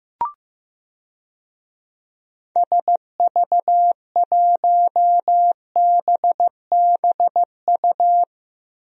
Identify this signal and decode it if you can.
Morse code, heard as a keyed tone, about 15 wpm, 700 Hz. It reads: SV1BBU
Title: DXCC-100 Call Signs - 6 Characters - Repeated 2x - 15wpm